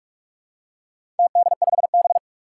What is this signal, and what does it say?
Morse code, heard as a keyed tone, about 46 wpm, 700 Hz. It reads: TD5B
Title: Four Challenging Letter-Number Combinations 45wpm